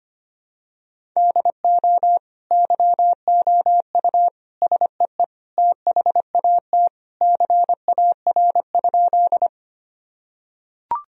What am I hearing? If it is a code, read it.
Morse code, heard as a keyed tone, about 25 wpm, 700 Hz. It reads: DO YOU SEE THAT CAR?
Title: Sentences from Top 1000 Words - Review Part 1 - 25wpm